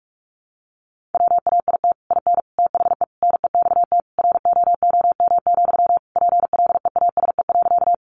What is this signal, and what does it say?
Morse code, heard as a keyed tone, about 44 wpm, 700 Hz. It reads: WAIT IN THE NEXT ROOM, PLEASE.
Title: Sentences from Top 1500 Words 45wpm